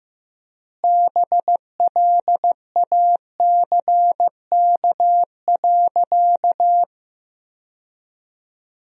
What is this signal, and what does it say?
Morse code, heard as a keyed tone, about 15 wpm, 700 Hz. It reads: BLACK.
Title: 3x Word-Spacing - Sentences from Top 400 Words - 15wpm